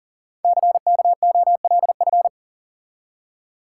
Morse code, heard as a keyed tone, about 42 wpm, 700 Hz. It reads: CKOLF
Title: Code Groups - 5 Letters - 40wpm